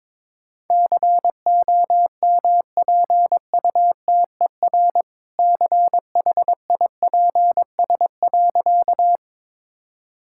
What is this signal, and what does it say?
Morse code, heard as a keyed tone, about 22 wpm, 700 Hz. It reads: COMPUTER CHIPS.
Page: morse-code-ninja-practice-22wpm